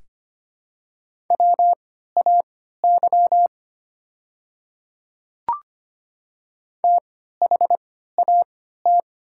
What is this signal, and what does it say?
Morse code, heard as a keyed tone, about 25 wpm, spaced as for 14 wpm, 700 Hz. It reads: WAY THAT
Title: 3x Character Spacing - Top 300 Words - 25wpm